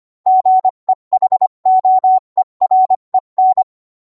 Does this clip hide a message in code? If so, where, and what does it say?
Yes: Morse code from 0.3 to 3.6 s, heard as a keyed tone, about 25 wpm, 750 Hz. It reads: GEHOEREN